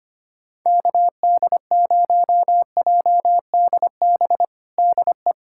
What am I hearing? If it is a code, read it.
Morse code, heard as a keyed tone, about 25 wpm, 700 Hz. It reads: KD0JDB DE
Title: USA Callsign DE Callsign - 6 Characters - 25wpm